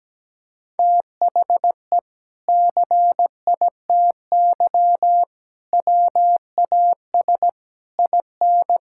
Morse code, heard as a keyed tone, about 17 wpm, 700 Hz. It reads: THE CITY WAS IN
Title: Sentences from Top 1600 Words 17wpm